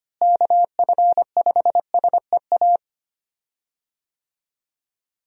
Morse code, heard as a keyed tone, about 25 wpm, 700 Hz. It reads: KF5SEA
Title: USA Call Signs - 6 Characters - Repeated 3x - 25wpm